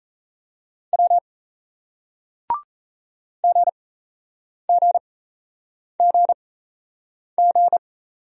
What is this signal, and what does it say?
Morse code, heard as a keyed tone, about 38 wpm, spaced as for 8 wpm, 700 Hz. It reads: W GGGG